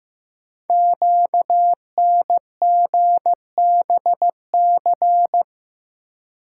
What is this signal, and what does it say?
Morse code, heard as a keyed tone, about 15 wpm, 700 Hz. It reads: QNGBC
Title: Code Groups - 5 Letters - 15wpm